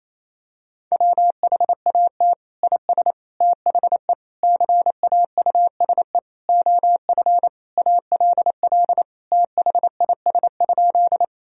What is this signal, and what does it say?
Morse code, heard as a keyed tone, about 28 wpm, 700 Hz. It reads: WHAT IS THE CAUSE OF ALL THIS?
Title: Sentences from Top 1000 Words - Review Part 1 - 28wpm